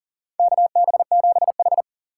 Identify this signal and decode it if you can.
Morse code, heard as a keyed tone, about 40 wpm, 700 Hz. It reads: KB7H
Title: K1USN SST QSO Practice - 40wpm